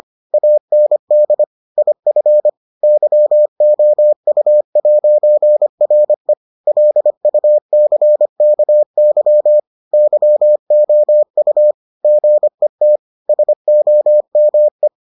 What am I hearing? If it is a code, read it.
Morse code, heard as a keyed tone, about 25 wpm, 600 Hz. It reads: AND IF YOU'RE LUCKY YOU GET SOME